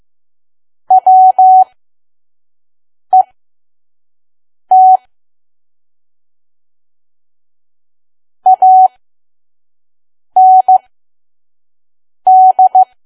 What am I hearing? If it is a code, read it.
Morse code, heard as a keyed tone, about 15 wpm, spaced as for 5 wpm, 750 Hz. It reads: WET AND
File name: Archive / 5 WPM files / PM